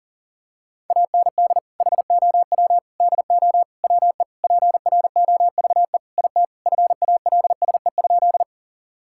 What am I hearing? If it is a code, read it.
Morse code, heard as a keyed tone, about 37 wpm, 700 Hz. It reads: AND HOW DO WE PROVE IT FALSE?
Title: Sentences from Top 1600 Words 40wpm